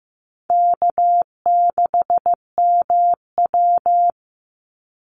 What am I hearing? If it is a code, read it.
Morse code, heard as a keyed tone, about 15 wpm, 700 Hz. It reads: K6MW
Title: NA SOTA Chaser Callsigns 15wpm